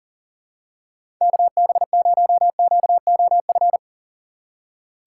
Morse code, heard as a keyed tone, about 40 wpm, 700 Hz. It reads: KB0QOF
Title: USA Call Signs - 6 Characters - 6x Character Spacing - 40wpm